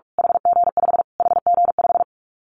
Morse code, heard as a keyed tone, about 46 wpm, 700 Hz. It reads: HG5 HG5
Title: Callsign Prefixes for Top 25 DXCC - Repeated 3x - 45wpm